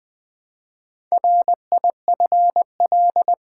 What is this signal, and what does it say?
Morse code, heard as a keyed tone, about 20 wpm, 700 Hz. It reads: RIFL
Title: Build A Word - 1 Letter At A Time - 3 to 5-Letter Words - 20wpm